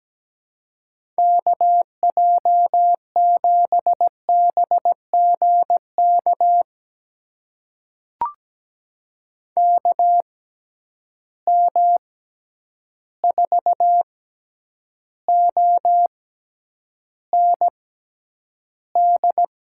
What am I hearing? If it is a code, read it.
Morse code, heard as a keyed tone, about 17 wpm, spaced as for 9 wpm, 700 Hz. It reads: KJ7BGK KM4OND
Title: USA Call Signs - 6 Characters - 6x Character Spacing - 17wpm